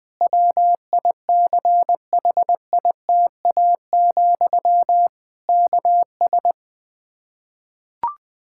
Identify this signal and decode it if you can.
Morse code, heard as a keyed tone, about 21 wpm, 700 Hz. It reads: WICHITA, KS
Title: K1USN SST QSO Practice 20wpm